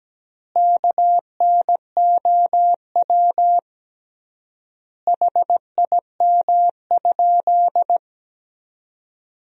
Morse code, heard as a keyed tone, about 17 wpm, 700 Hz. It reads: KNOW HIM?
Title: 3x Word-Spacing - Sentences from Top 100 Words - 17wpm